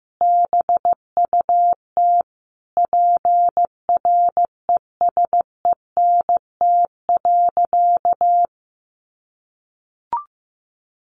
Morse code, heard as a keyed tone, about 14 wpm, 700 Hz. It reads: BUT PRESENT.
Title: Sentences from Top 1300 Words 15wpm